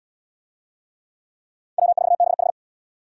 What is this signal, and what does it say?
Morse code, heard as a keyed tone, about 77 wpm, 700 Hz. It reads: W4BH